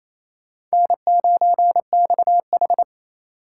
Morse code, heard as a keyed tone, about 35 wpm, 700 Hz. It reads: N9XH
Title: Speed Racing - USA 1x2 Call Signs 28wpm